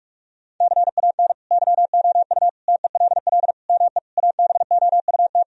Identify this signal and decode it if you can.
Morse code, heard as a keyed tone, about 41 wpm, 700 Hz. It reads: CAN YOU TELL ME ABOUT
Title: Sentences from Top 1000 Words - Review Part 3 - 45wpm